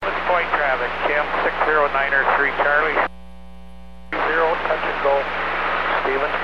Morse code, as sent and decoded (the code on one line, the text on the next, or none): none